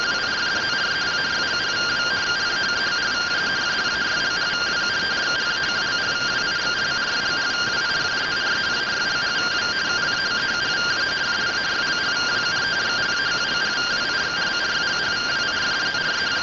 Two-tone teleprinter signal, about 170 Hz shift.
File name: findings / 100 surview 0.gen-33 rtty rx